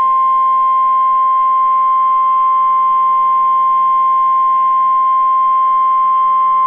Emergency Alert System (SAME) tones (the 1050 Hz attention tone). It is heard from 1.2 to 5.7 s.